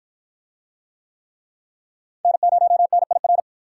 Morse code, heard as a keyed tone, about 53 wpm, 700 Hz. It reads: N0NIR